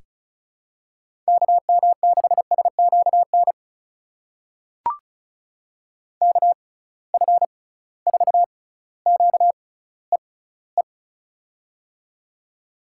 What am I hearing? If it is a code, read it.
Morse code, heard as a keyed tone, about 34 wpm, spaced as for 18 wpm, 700 Hz. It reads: KM6SQN KF4QEE
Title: USA Call Signs - 6x Character Spacing - 35wpm